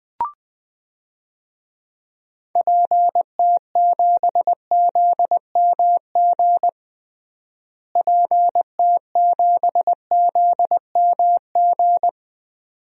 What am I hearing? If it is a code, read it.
Morse code, heard as a keyed tone, about 21 wpm, 700 Hz. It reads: PT7ZMG PT7ZMG
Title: DXCC-100 Call Signs - 6 Characters - Repeated 2x - 20wpm